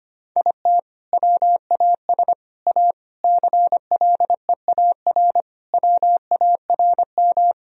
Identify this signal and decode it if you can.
Morse code, heard as a keyed tone, about 25 wpm, 700 Hz. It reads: IT WAS A CLEAR WARM
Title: Sentences from Top 1000 Words - Review Part 2 - 25wpm